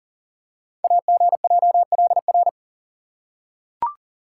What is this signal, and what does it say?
Morse code, heard as a keyed tone, about 40 wpm, 690 Hz. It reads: AGJLR
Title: Code Groups - 5 Letters - 40wpm